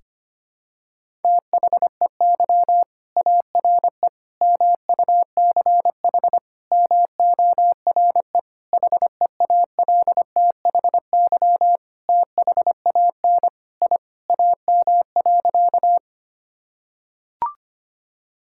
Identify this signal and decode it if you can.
Morse code, heard as a keyed tone, about 25 wpm, 700 Hz. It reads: THEY ARE MUCH MORE HEALTHY THAN I AM.
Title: Sentences from Top 1000 Words - Review Part 5 - 25wpm